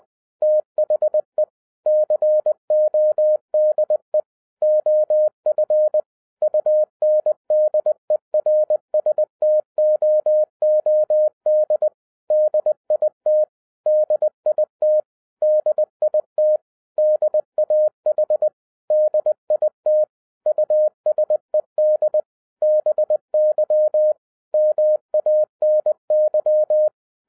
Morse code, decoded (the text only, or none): THE CODE OF UNDERSTOOD DIT DIT DIT DAH DIT USED BY MANY